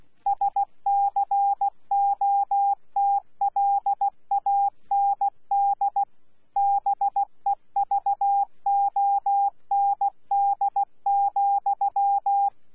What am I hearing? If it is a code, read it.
Morse code, heard as a keyed tone, about 16 wpm, 800 Hz. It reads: SCOTLAND BEVOND,
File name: Z PI4VRZ / PI4VRZ 14-jun-2025